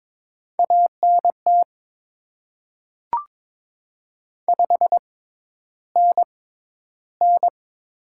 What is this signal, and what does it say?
Morse code, heard as a keyed tone, about 23 wpm, spaced as for 11 wpm, 700 Hz. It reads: ANT 5NN
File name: Top 20 QSO Elements - 6x Character Spacing - 22wpm